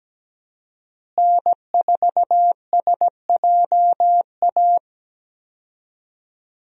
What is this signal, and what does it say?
Morse code, heard as a keyed tone, about 17 wpm, 700 Hz. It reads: N4SJA
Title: USA Call Signs - 5 Characters - Repeated 2x - 17wpm